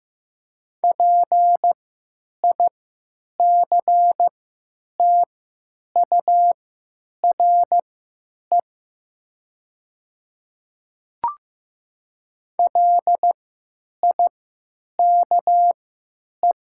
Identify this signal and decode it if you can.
Morse code, heard as a keyed tone, about 15 wpm, spaced as for 9 wpm, 700 Hz. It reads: PICTURE LIKE
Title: 3x Character Spacing - Top 200 Words - 15wpm